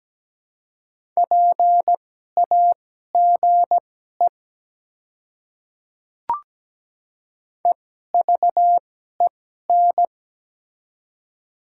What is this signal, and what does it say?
Morse code, heard as a keyed tone, about 17 wpm, spaced as for 12 wpm, 690 Hz. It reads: PAGE EVEN
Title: 2x Character Spacing - Top 300 Words - 17wpm